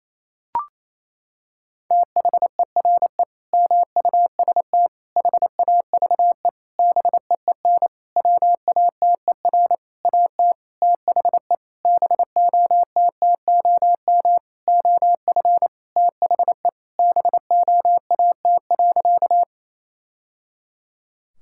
Morse code, heard as a keyed tone, about 28 wpm, 700 Hz. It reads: THERE MUST HAVE BEEN WATER AT THE BOTTOM OF THE BOAT.
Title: Sentences from Top 1000 Words - Review Part 6 - 28wpm